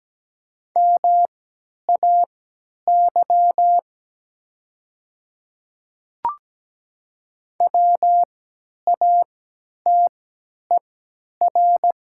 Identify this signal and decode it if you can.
Morse code, heard as a keyed tone, about 17 wpm, spaced as for 10 wpm, 700 Hz. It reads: MAY WATER
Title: 3x Character Spacing - Top 100 Words - 17wpm